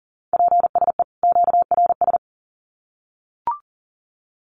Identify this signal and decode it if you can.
Morse code, heard as a keyed tone, about 40 wpm, 700 Hz. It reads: PSE QRS
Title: Common QSO Phrases 40wpm